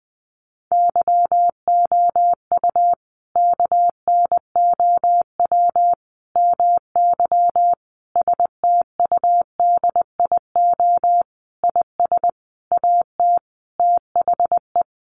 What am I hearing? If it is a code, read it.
Morse code, heard as a keyed tone, about 19 wpm, 700 Hz. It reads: YOU KNOW MY STUDIO IS AT THE
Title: Sentences from Top 1200 Words 20wpm